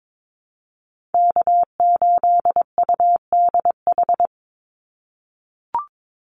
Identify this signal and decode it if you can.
Morse code, heard as a keyed tone, about 22 wpm, 700 Hz. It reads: K8UDH
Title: Call Signs - 22wpm